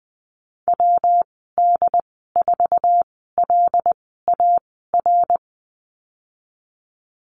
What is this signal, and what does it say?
Morse code, heard as a keyed tone, about 20 wpm, spaced as for 15 wpm, 700 Hz. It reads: WD4LAR
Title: 140 Lesson 23g - Letter f - Call Signs 20wpm